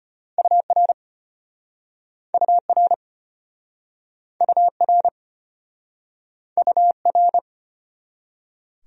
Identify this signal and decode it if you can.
Morse code, heard as a keyed tone, about 34 wpm, 700 Hz. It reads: UR UR UR UR